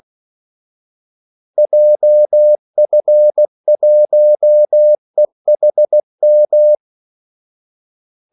Morse code, heard as a keyed tone, about 16 wpm, 600 Hz. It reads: JF1EHM
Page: Kaldesignaler 16 wpm